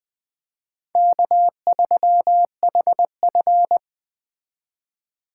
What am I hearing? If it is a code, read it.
Morse code, heard as a keyed tone, about 20 wpm, 700 Hz. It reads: K3HF
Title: Speed Racing - USA Call Signs 17wpm